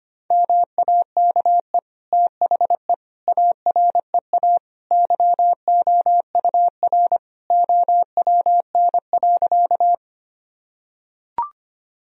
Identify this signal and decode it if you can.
Morse code, heard as a keyed tone, about 25 wpm, 700 Hz. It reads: MAKE THE AREA YOUR OWN.
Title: Sentences from Top 1000 Words - Review Part 2 - 25wpm